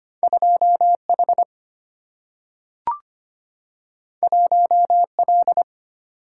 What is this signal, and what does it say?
Morse code, heard as a keyed tone, about 25 wpm, 700 Hz. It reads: 2H 1L
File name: Two Letter-Number Combinations - 25wpm